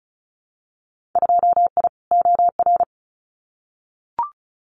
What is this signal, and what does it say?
Morse code, heard as a keyed tone, about 34 wpm, 700 Hz. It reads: 2I OR
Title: Winter Field Day Exchanges Encore 35wpm